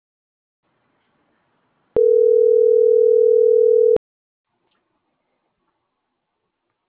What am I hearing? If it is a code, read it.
Ringback tone (North America), 440 + 480 Hz.